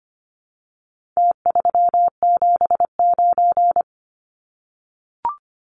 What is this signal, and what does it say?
Morse code, heard as a keyed tone, about 25 wpm, 700 Hz. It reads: T379
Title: Numbers - Contest Style - 4-Digit 25wpm